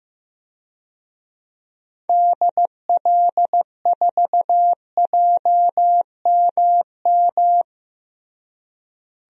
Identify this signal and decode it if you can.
Morse code, heard as a keyed tone, about 15 wpm, 700 Hz. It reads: DL4JMM
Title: DXCC-100 Call Signs - 6 Characters - Repeated 2x - 15wpm